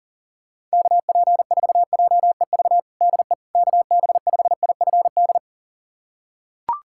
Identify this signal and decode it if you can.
Morse code, heard as a keyed tone, about 40 wpm, 700 Hz. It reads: KP4JEV DE KB5IFD
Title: USA Callsign DE Callsign 40wpm